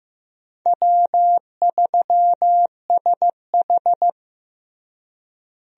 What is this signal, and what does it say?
Morse code, heard as a keyed tone, about 16 wpm, 700 Hz. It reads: W3SH
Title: USA Call Signs - 4 Characters - Repeated 3x - 15wpm